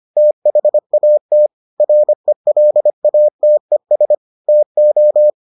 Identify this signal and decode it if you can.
Morse code, heard as a keyed tone, about 25 wpm, 600 Hz. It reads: THAT RELATES TO